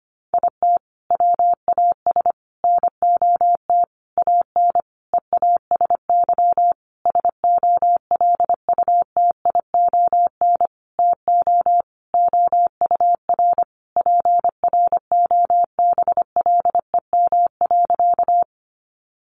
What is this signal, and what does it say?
Morse code, heard as a keyed tone, about 25 wpm, 700 Hz. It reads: IT WAS NOT AN EASY SOLUTION TO OUR PROBLEM.
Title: Sentences from Top 1300 Words 25wpm